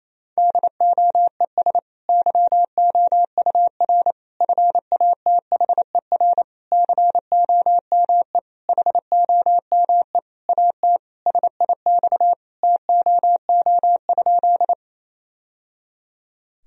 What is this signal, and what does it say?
Morse code, heard as a keyed tone, about 28 wpm, 700 Hz. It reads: DOES YOUR FATHER COME HOME AT SIX TOO?
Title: Sentences from Top 1000 Words - Review Part 2 - 28wpm